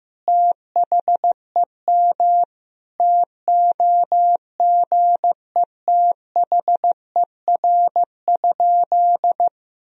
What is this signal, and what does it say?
Morse code, heard as a keyed tone, about 16 wpm, 700 Hz. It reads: THEM TOGETHER?
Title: Sentences from Top 1000 Words - Review Part 1 - 15wpm